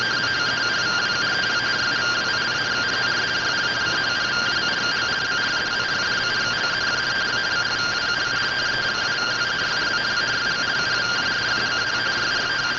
Two-tone teleprinter signal, about 170 Hz shift.
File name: findings / SAGITAL fr15 nb203 3mm.gen-38 rtty rx